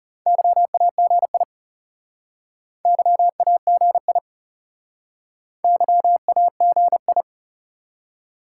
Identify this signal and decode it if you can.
Morse code, heard as a keyed tone, about 37 wpm, 700 Hz. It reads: YAGI YAGI YAGI
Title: Speed-Racing Top 180 QSO Elements 30wpm